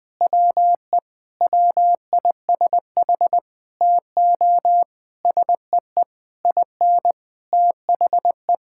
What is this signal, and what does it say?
Morse code, heard as a keyed tone, about 20 wpm, 700 Hz. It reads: WE WISH TO SEE IN THE
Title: Sentences with 9-Word Phrases - 20wpm